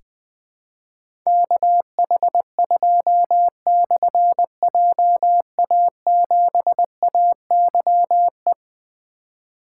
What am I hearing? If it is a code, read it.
Morse code, heard as a keyed tone, about 21 wpm, 700 Hz. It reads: KH2/JA7AYE
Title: Difficult Call Signs 20wpm